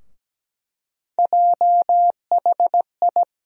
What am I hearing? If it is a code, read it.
Morse code, heard as a keyed tone, about 17 wpm, 700 Hz. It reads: JHI
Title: USA Call Sign Postfixes - 3 Letters 17wpm